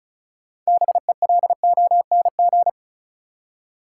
Morse code, heard as a keyed tone, about 34 wpm, 700 Hz. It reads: BELONG
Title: Six Letter Words - 35wpm